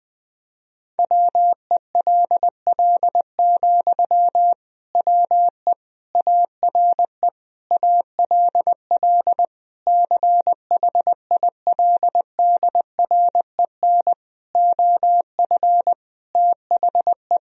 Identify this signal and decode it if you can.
Morse code, heard as a keyed tone, about 20 wpm, 700 Hz. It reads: WELL, WE ARE ALL CHILDREN OF THE
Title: Sentences from Top 1000 Words - Review Part 4 - 20wpm